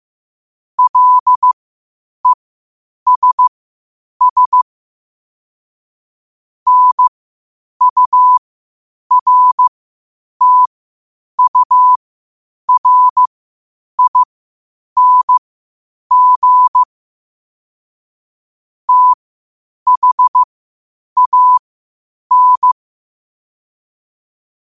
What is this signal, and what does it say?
Morse code, heard as a keyed tone, about 15 wpm, spaced as for 8 wpm, 1000 Hz. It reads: LESS NURTURING THAN
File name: qotdfiles / 8wpm